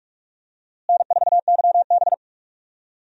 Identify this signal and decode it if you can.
Morse code, heard as a keyed tone, about 46 wpm, 700 Hz. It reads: N4YB
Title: USA Call Signs - 4 Characters - Repeated 3x - 45wpm